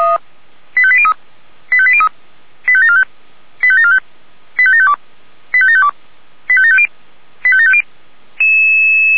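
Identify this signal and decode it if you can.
Five-tone selective call (ZVEI), heard as quick runs of five tones, tones of about 70 ms. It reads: 85803 85803 85745 85745 85752 85752 85750 85750
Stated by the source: ZVEI five-tone call received by SDR.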